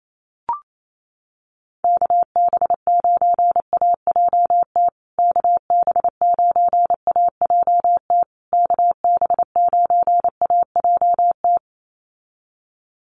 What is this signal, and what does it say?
Morse code, heard as a keyed tone, about 28 wpm, 700 Hz. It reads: KB9AJT KB9AJT KB9AJT
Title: USA Call Signs Repeated 3x 28wpm